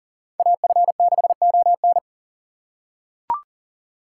Morse code, heard as a keyed tone, about 40 wpm, 700 Hz. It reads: AF6ON
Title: USA Call Signs - 5 Characters - 3x Character Spacing - 40wpm